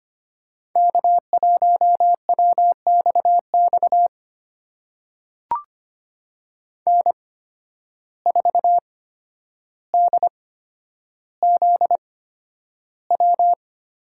Morse code, heard as a keyed tone, about 25 wpm, spaced as for 11 wpm, 700 Hz. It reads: K1WXX N4DZW
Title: USA Call Signs - 5 Characters - 8x Character Spacing - 25wpm